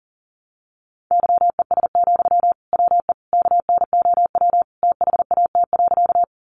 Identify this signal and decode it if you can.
Morse code, heard as a keyed tone, about 38 wpm, 700 Hz. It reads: YES, WE KNOW THAT.
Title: Sentences from Top 500 Words 40wpm